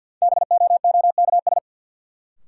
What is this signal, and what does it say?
Morse code, heard as a keyed tone, about 51 wpm, 700 Hz. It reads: BOOKS
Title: Sentences from Top 1000 Words - Review Part 4 - New Word Drill - 50wpm